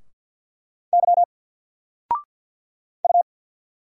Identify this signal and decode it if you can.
Morse code, heard as a keyed tone, about 50 wpm, 700 Hz. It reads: Y U
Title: Single Letter-Number Alt 50wpm